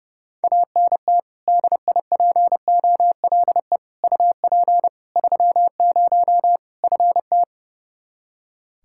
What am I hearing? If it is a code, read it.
Morse code, heard as a keyed tone, about 30 wpm, 700 Hz. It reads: ANT DIPOLE UP 30 FT
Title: K1USN SST QSO Practice 30wpm